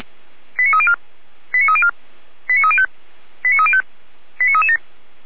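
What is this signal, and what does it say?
Five-tone selective call (ZVEI), heard as quick runs of five tones, tones of about 70 ms. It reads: 89304 89304 89305 89305 89337
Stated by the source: Received by SDR as a ZVEI five-tone call.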